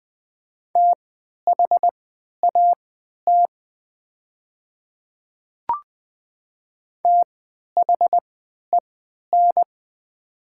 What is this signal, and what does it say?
Morse code, heard as a keyed tone, about 20 wpm, spaced as for 11 wpm, 700 Hz. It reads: THAT THEN
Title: 3x Character Spacing - Top 100 Words - 20wpm